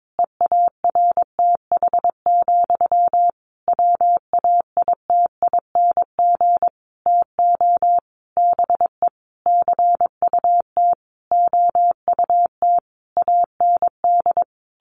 Morse code, heard as a keyed tone, about 22 wpm, 700 Hz. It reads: EARTH, WAITING TO BE CUT OUT AND
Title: A Sound Of Thunder 22wpm